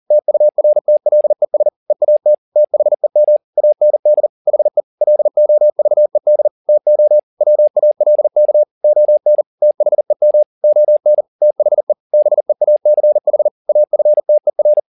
Morse code, heard as a keyed tone, about 40 wpm, 600 Hz. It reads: TURTLES EAT THEM AND HE LOVED TO WALK ON THEM ON THE BEACH AFTER